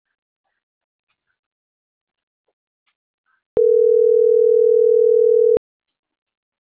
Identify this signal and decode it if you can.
Ringback tone (North America), 440 + 480 Hz.